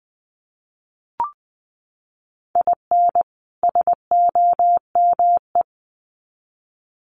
Morse code, heard as a keyed tone, about 20 wpm, 700 Hz. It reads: IN SOME
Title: Sets of 2 Words 20wpm